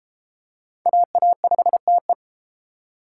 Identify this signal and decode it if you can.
Morse code, heard as a keyed tone, about 31 wpm, 700 Hz. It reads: AA5TE